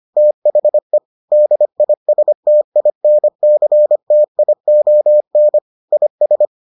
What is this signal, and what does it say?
Morse code, heard as a keyed tone, about 25 wpm, 600 Hz. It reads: THE DISTINCTION IS